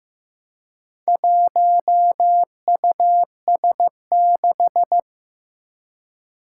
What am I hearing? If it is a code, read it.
Morse code, heard as a keyed tone, about 15 wpm, 700 Hz. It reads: 1US6
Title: Four Challenging Letter-Number Combinations 15wpm